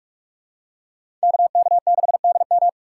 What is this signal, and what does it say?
Morse code, heard as a keyed tone, about 45 wpm, 700 Hz. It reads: KK6DM
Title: USA Call Signs - 5 Characters - 3x Character Spacing - 45wpm